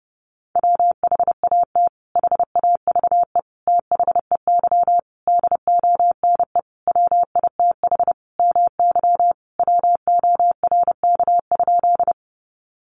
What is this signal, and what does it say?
Morse code, heard as a keyed tone, about 30 wpm, 700 Hz. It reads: WHAT HAVE THEY DONE WITH MY WORK?
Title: Sentences from Top 500 Words 30wpm